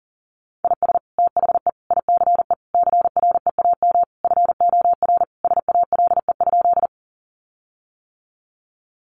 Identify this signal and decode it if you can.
Morse code, heard as a keyed tone, about 40 wpm, 700 Hz. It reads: IS THE ICE CREAM FOR SALE?
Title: Sentences from Top 1500 Words 40wpm